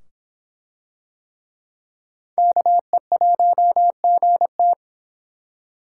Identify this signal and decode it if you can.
Morse code, heard as a keyed tone, about 25 wpm, 700 Hz. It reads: KE1GT